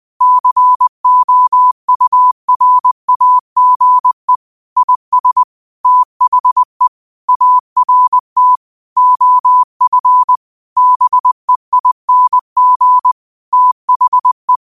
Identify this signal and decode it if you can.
Morse code, heard as a keyed tone, about 20 wpm, 1000 Hz. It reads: COURAGE IS THE ART OF BEING THE
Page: Quotes for Tue, 12 Aug 2025 in Morse Code at 20 words per minute.